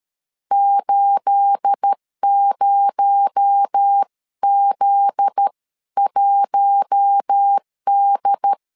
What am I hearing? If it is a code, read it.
Morse code, heard as a keyed tone, about 13 wpm, 800 Hz. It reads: 80Z1D
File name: CW / DET / 1 9 1500C S B 12wpm 3819